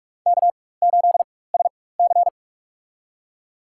Morse code, heard as a keyed tone, about 46 wpm, spaced as for 21 wpm, 700 Hz. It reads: K8SC
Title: USA Call Signs - 4 Characters - 4x Character Spacing - 45wpm